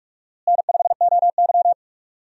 Morse code, heard as a keyed tone, about 46 wpm, 700 Hz. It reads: N5OY
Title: USA Call Signs - 4 Characters - Repeated 2x - 45wpm